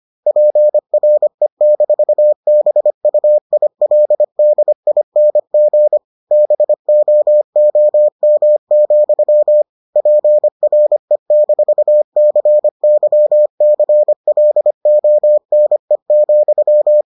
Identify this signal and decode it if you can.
Morse code, heard as a keyed tone, about 25 wpm, 600 Hz. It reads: PRE-BUILDING BOOM, PRE-CYCLONE,